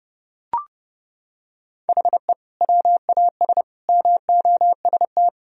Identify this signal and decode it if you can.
Morse code, heard as a keyed tone, about 30 wpm, 700 Hz. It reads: HE WAS MOST
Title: Sentences from Top 1000 Words - Review Part 4 - 30wpm